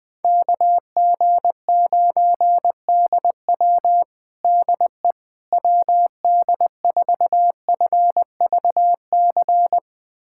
Morse code, heard as a keyed tone, about 20 wpm, 700 Hz. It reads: KG9DW DE WD4FVC
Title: USA Callsign DE Callsign 20wpm